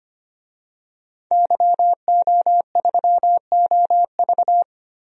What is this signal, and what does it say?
Morse code, heard as a keyed tone, about 25 wpm, 700 Hz. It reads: YO3OV
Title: DXCC-100 Call Signs - 5 Characters - Repeated 3x - 25wpm